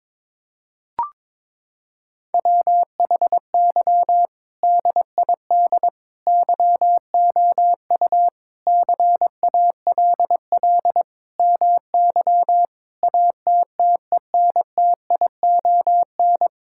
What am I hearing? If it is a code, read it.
Morse code, heard as a keyed tone, about 22 wpm, 700 Hz. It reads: WHY DID YOU CALL MY ATTENTION
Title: Sentences from Top 1000 Words - Review Part 6 - 22wpm